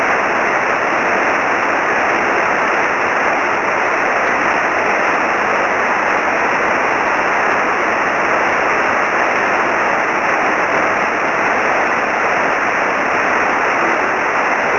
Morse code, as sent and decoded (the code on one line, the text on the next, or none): none